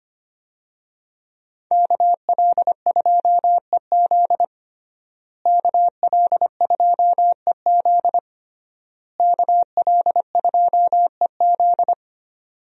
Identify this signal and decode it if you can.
Morse code, heard as a keyed tone, about 25 wpm, 700 Hz. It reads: KL2EZ KL2EZ KL2EZ
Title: USA Call Signs - 5 Characters - Repeated 3x - 25wpm